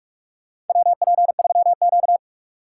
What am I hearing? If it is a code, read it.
Morse code, heard as a keyed tone, about 42 wpm, 700 Hz. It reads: WP3Q
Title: USA Call Signs - 3x Character Spacing - 45wpm